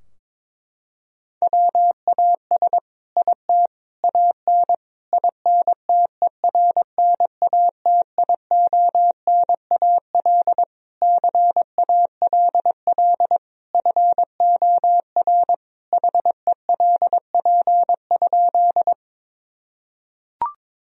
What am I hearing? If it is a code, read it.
Morse code, heard as a keyed tone, about 22 wpm, 700 Hz. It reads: WAS IT AN INTERNATIONAL CALL FOR HELP?
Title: Sentences from Top 1000 Words - Review Part 4 - 22wpm